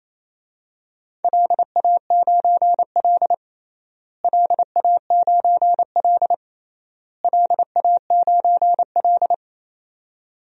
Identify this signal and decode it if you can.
Morse code, heard as a keyed tone, about 28 wpm, 700 Hz. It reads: LA9L LA9L LA9L
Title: DXCC-100 Call Signs - 4 Characters - Repeated 3x - 28wpm